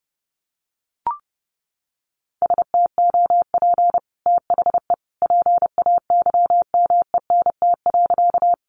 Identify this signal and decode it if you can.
Morse code, heard as a keyed tone, about 30 wpm, 700 Hz. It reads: STOP THE PAYMENT.
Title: Sentences from Top 1200 Words 30wpm